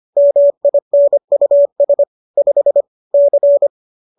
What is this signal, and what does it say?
Morse code, heard as a keyed tone, about 25 wpm, 570 Hz. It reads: MINUS 5 C
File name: qso203 25